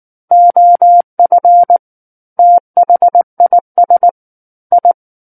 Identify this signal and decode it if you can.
Morse code, heard as a keyed tone, about 19 wpm, 700 Hz. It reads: OF THIS I
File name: Book One / 10-Chapter 10 - 19 WPM